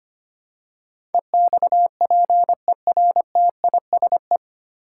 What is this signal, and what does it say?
Morse code, heard as a keyed tone, about 25 wpm, 700 Hz. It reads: EXPERTISE
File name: Nine Letter Words - 25wpm